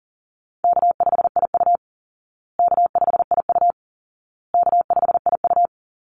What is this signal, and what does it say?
Morse code, heard as a keyed tone, about 40 wpm, 700 Hz. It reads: K5IU K5IU K5IU
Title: Easy Call Signs Repeated 3x - 40wpm